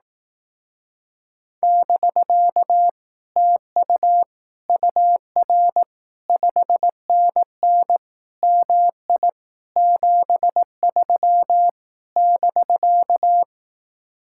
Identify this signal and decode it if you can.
Morse code, heard as a keyed tone, about 18 wpm, 700 Hz. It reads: <BK> TU UR 5NN MI 73 <BK>
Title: Speed-Racing Common QSO Phrases Encore - 15wpm